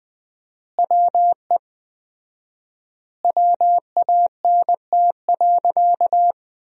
Morse code, heard as a keyed tone, about 20 wpm, 700 Hz. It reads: WE WANT.
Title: 4x Word-Spacing - Sentences from Top 200 Words - 20wpm